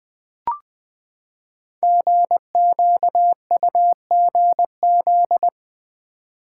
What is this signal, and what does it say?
Morse code, heard as a keyed tone, about 20 wpm, 700 Hz. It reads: GQUGZ
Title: Code Groups - 5 Letters - 20wpm